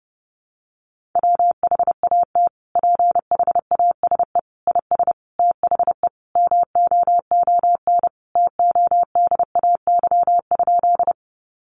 Morse code, heard as a keyed tone, about 30 wpm, 700 Hz. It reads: WHAT PHASE IS THE MOON TODAY?
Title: Sentences from Top 1200 Words 30wpm